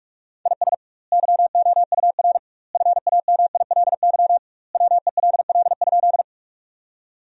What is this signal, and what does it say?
Morse code, heard as a keyed tone, about 42 wpm, 700 Hz. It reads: IS YOUR FAMILY WELL?
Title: 2x Word-Spacing - Sentences from Top 400 Words - 45wpm